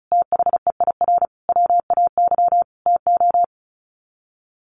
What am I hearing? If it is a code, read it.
Morse code, heard as a keyed tone, about 34 wpm, 700 Hz. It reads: THEIR WAY TO
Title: Sets of 5 Words Encore2 35wpm